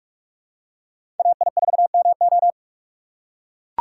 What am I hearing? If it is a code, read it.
Morse code, heard as a keyed tone, about 43 wpm, 700 Hz. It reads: AI4MO
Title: USA Call Signs - 5 Characters - 4x Character Spacing - 45wpm